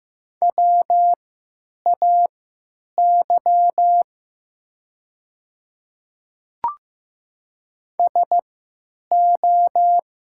Morse code, heard as a keyed tone, about 15 wpm, spaced as for 8 wpm, 700 Hz. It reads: WAY SO
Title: 3x Character Spacing - Top 200 Words - 15wpm